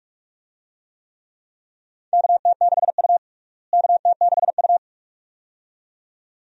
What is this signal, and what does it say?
Morse code, heard as a keyed tone, about 42 wpm, 700 Hz. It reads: KT6U KT6U
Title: USA Call Signs - 4 Characters - Repeated 2x - 45wpm